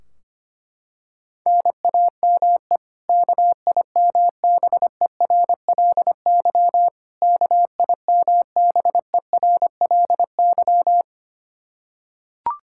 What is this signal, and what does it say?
Morse code, heard as a keyed tone, about 25 wpm, 700 Hz. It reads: NAME KIMBERLY KIMBERLY
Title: Practice QSO - Exchange 1 - Beginning - 25wpm